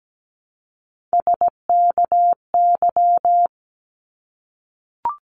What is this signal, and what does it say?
Morse code, heard as a keyed tone, about 17 wpm, 700 Hz. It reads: SKY
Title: Multi-letter Phonemes 17wpm